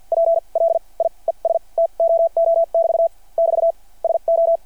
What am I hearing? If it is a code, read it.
Morse code, heard as a keyed tone, about 50 wpm, spaced as for 35 wpm, 660 Hz. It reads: PLIES TOO <BT> <BT> SO
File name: archived / CRSnet